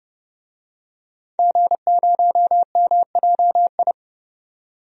Morse code, heard as a keyed tone, about 29 wpm, 700 Hz. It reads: G0MJI
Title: DXCC-100 Call Signs - 5 Characters - 30wpm